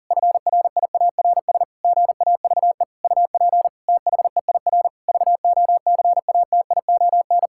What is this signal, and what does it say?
Morse code, heard as a keyed tone, about 40 wpm, 700 Hz. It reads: FRIARS GAVE UP THEIR VOCATION
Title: Pangram Challenge v2 - 40wpm